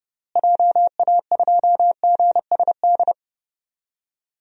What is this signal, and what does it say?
Morse code, heard as a keyed tone, about 31 wpm, 700 Hz. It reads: JA2GSD
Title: DXCC-100 Call Signs - 6 Characters - Repeated 2x - 30wpm